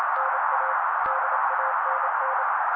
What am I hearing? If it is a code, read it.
Morse code, heard as a keyed tone, about 28 wpm, 530 Hz. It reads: NA DANN
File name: daily / 27wpm noise